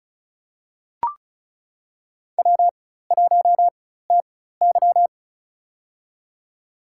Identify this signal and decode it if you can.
Morse code, heard as a keyed tone, about 36 wpm, spaced as for 16 wpm, 700 Hz. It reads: W1TY
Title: USA Call Signs - 4 Characters - 4x Character Spacing - 35wpm